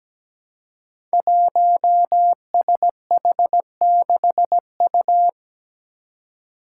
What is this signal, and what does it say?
Morse code, heard as a keyed tone, about 17 wpm, 700 Hz. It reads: 1SH6U
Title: Five Challenging Letter-Number Combinations 17wpm